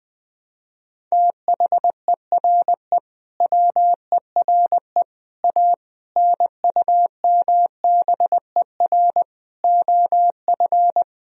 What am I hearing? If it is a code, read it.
Morse code, heard as a keyed tone, about 20 wpm, 700 Hz. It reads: THERE WERE A NUMBER OF
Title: Sentences from Top 1000 Words - Review Part 1 - 20wpm